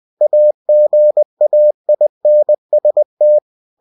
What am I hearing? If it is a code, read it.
Morse code, heard as a keyed tone, about 20 wpm, 600 Hz. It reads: AGAINST
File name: cw-books / winnie-the-pooh / winnie-20wpm-page-0076